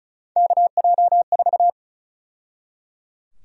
Morse code, heard as a keyed tone, about 34 wpm, 700 Hz. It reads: KJ4
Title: USA Call Sign Prefixes - 35wpm